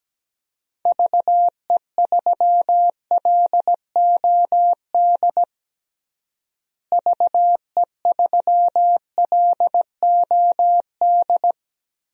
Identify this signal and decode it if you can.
Morse code, heard as a keyed tone, about 17 wpm, 700 Hz. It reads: VE3LOD VE3LOD
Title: DXCC-100 Call Signs - 6 Characters - Repeated 2x - 17wpm